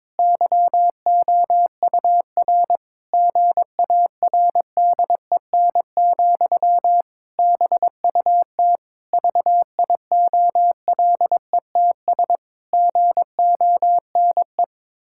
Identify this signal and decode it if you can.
Morse code, heard as a keyed tone, about 22 wpm, 700 Hz. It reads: YOUR GARDEN, BUT VIOLETS GONE
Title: Pangram Challenge v2 - 22wpm